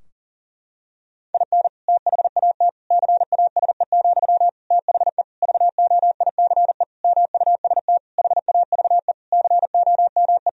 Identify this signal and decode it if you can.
Morse code, heard as a keyed tone, about 37 wpm, 700 Hz. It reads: IN THAT CASE, THE VOICE MUST HAVE COME
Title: Sentences from Top 1000 Words - Review Part 2 - 40wpm